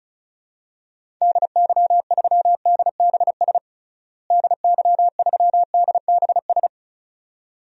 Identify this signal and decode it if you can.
Morse code, heard as a keyed tone, about 36 wpm, 700 Hz. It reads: DY3DBS DY3DBS
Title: DXCC-100 Call Signs - 6 Characters - Repeated 2x - 35wpm